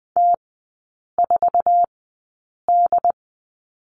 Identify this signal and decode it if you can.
Morse code, heard as a keyed tone, about 20 wpm, spaced as for 8 wpm, 700 Hz. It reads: T4D
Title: 074 Lesson 13e - Letter L - Three Characters 20wpm